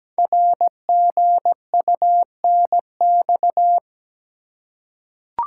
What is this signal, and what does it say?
Morse code, heard as a keyed tone, about 17 wpm, 700 Hz. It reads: RGUNX
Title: Code Groups - 5 Letters - 17wpm